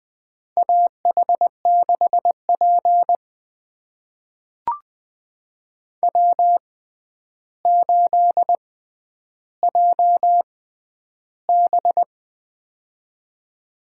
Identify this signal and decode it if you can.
Morse code, heard as a keyed tone, about 20 wpm, spaced as for 10 wpm, 700 Hz. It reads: AH6P W8JB
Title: USA Call Signs - 4 Characters - 6x Character Spacing - 20wpm